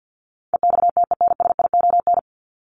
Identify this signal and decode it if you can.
Morse code, heard as a keyed tone, about 50 wpm, 700 Hz. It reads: EXTENSION
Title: Top 1501-1750 Words 50wpm